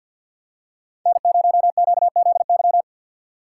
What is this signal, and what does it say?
Morse code, heard as a keyed tone, about 51 wpm, 700 Hz. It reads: N0QZY
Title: USA Call Signs - 5 Characters - 6x Character Spacing - 50wpm